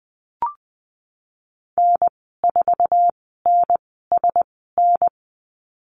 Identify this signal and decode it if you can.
Morse code, heard as a keyed tone, about 20 wpm, spaced as for 15 wpm, 700 Hz. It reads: N4NSN
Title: 161 Lesson 26g - Letter g - Call Signs 20wpm